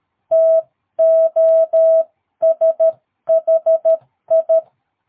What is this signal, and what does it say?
Morse code, heard as a keyed tone, about 13 wpm, 660 Hz. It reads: TOSHI
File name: jh8cbh / 2022cw